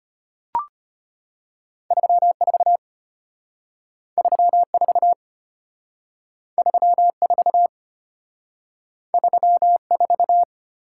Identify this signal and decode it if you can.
Morse code, heard as a keyed tone, about 32 wpm, 700 Hz. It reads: 34 34 34 34